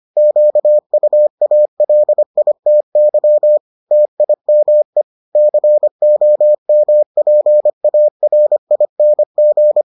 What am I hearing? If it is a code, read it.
Morse code, heard as a keyed tone, about 25 wpm, 600 Hz. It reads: QUALITY TIME COMPARING